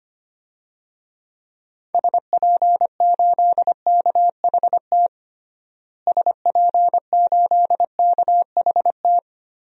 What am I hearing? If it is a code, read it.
Morse code, heard as a keyed tone, about 25 wpm, 700 Hz. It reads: SP8KHT SP8KHT
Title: DXCC-100 Call Signs - 6 Characters - Repeated 2x - 25wpm